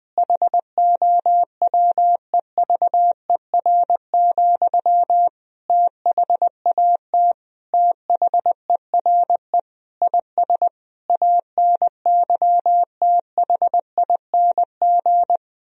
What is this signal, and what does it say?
Morse code, heard as a keyed tone, about 21 wpm, 700 Hz. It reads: HOWEVER, THAT THERE IS ANYTHING
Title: Sentences from Top 1000 Words - Review Part 3 - 20wpm